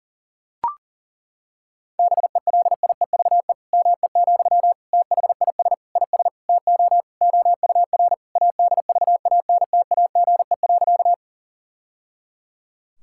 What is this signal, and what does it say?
Morse code, heard as a keyed tone, about 40 wpm, 700 Hz. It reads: BELIEVE ME, THIS IS TO OUR ADVANTAGE.
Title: Sentences from Top 1000 Words - Review Part 6 - 40wpm